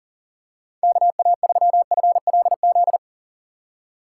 Morse code, heard as a keyed tone, about 38 wpm, 700 Hz. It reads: KA3FLZ
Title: USA Call Signs - 6 Characters - 2x Character Spacing - 40wpm